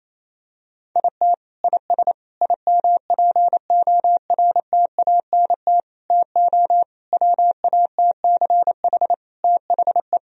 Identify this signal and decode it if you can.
Morse code, heard as a keyed tone, about 28 wpm, 700 Hz. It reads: IT IS IMPORTANT TO WATCH THE
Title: Sentences from Top 1000 Words - Review Part 4 - 28wpm